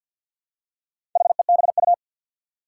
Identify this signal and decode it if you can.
Morse code, heard as a keyed tone, about 52 wpm, 700 Hz. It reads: HEBU
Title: Four Challenging Letter-Number Combinations 50wpm